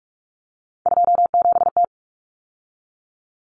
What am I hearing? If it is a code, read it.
Morse code, heard as a keyed tone, about 44 wpm, 700 Hz. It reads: 27T
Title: Common Cut Numbers - 3-Digit 45wpm